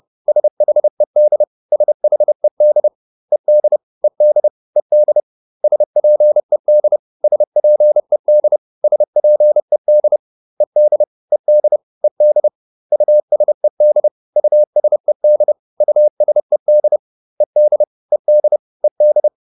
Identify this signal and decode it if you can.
Morse code, heard as a keyed tone, about 30 wpm, 600 Hz. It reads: SHED SHED ED ED ED SPED SPED SPED ED ED ED USED USED USED ED ED ED